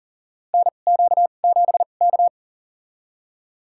Morse code, heard as a keyed tone, about 40 wpm, spaced as for 29 wpm, 700 Hz. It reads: NQ7K
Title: USA Call Signs - 2x Character Spacing - 40wpm